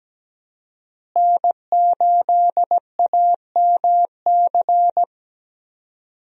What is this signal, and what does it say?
Morse code, heard as a keyed tone, about 17 wpm, 700 Hz. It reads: N8AMC
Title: USA Call Signs - 5 Characters - Repeated 3x - 17wpm